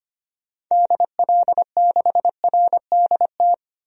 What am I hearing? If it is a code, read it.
Morse code, heard as a keyed tone, about 25 wpm, 700 Hz. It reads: DL6RDT
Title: DXCC-100 Call Signs - 6 Characters - 25wpm